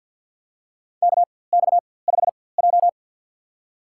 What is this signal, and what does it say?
Morse code, heard as a keyed tone, about 53 wpm, spaced as for 24 wpm, 700 Hz. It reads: KX5J
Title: USA Call Signs - 4x Character Spacing - 50wpm